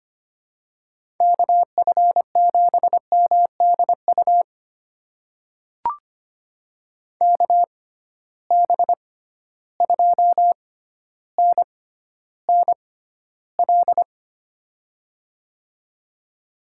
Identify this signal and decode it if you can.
Morse code, heard as a keyed tone, about 25 wpm, spaced as for 13 wpm, 700 Hz. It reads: KF7MDU KB2NNL
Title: USA Call Signs - 6 Characters - 6x Character Spacing - 25wpm